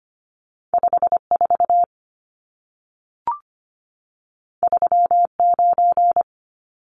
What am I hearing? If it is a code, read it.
Morse code, heard as a keyed tone, about 25 wpm, 700 Hz. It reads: 54 39
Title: Numbers - 2-Digit 25wpm